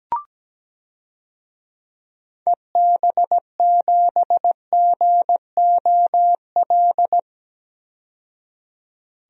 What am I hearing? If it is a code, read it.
Morse code, heard as a keyed tone, about 17 wpm, 700 Hz. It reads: EB7GOL
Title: DXCC-100 Call Signs - 6 Characters - 17wpm